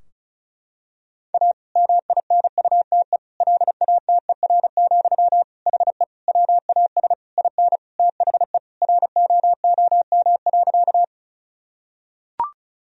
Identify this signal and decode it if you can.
Morse code, heard as a keyed tone, about 36 wpm, 700 Hz. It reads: A MINUTE LATER, HE WAS IN THE ROOM.
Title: Sentences from Top 1000 Words - Review Part 4 - 35wpm